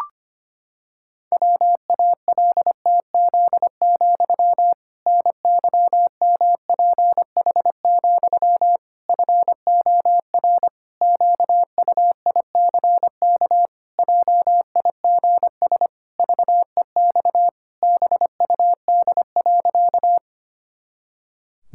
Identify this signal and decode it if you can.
Morse code, heard as a keyed tone, about 25 wpm, 700 Hz. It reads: WALTZ, NYMPH, FOR QUICK JIGS VEX BUD.
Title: Pangram Challenge v2 - 25wpm